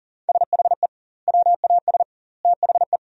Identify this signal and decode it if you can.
Morse code, heard as a keyed tone, about 37 wpm, 700 Hz. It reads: SHE WAS THE
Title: 2x Word-Spacing - Sentences from Top 200 Words - 40wpm